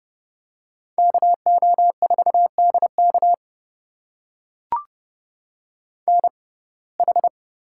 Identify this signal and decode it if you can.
Morse code, heard as a keyed tone, about 29 wpm, spaced as for 21 wpm, 700 Hz. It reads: KO4DK NH
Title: USA Call Signs - 6x Character Spacing - 30wpm